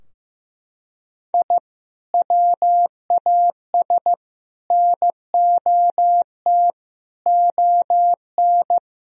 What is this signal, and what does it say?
Morse code, heard as a keyed tone, about 15 wpm, 700 Hz. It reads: I WAS NOT ON
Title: Sentences from Top 1000 Words - Review Part 4 - 15wpm